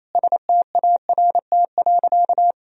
Morse code, heard as a keyed tone, about 28 wpm, 700 Hz. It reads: START.
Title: Sentences from Top 300 Words and Abbreviations 28wpm